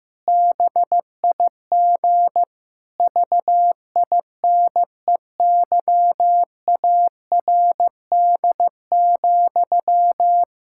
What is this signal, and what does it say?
Morse code, heard as a keyed tone, about 15 wpm, 700 Hz. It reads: BIG VINEYARD,
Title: Pangram Challenge v2 - 15wpm